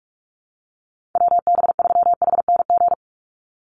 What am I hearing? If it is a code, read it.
Morse code, heard as a keyed tone, about 44 wpm, 700 Hz. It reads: WB3HNG
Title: USA Call Signs 45wpm